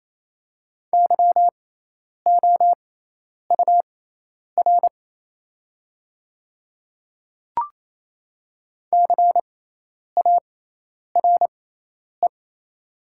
Morse code, heard as a keyed tone, about 28 wpm, spaced as for 10 wpm, 700 Hz. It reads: YOUR CARE
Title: 6x Character Spacing - Top 300 Words - 28wpm